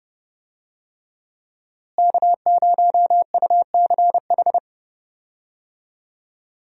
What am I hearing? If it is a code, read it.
Morse code, heard as a keyed tone, about 30 wpm, 700 Hz. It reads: K0UCH
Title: USA Call Signs - 5 Characters - Repeated 3x - 30wpm